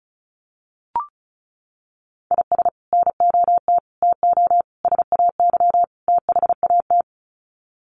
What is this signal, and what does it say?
Morse code, heard as a keyed tone, about 34 wpm, 700 Hz. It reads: IS NOT TO SAY THAT
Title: Sets of 5 Words 35wpm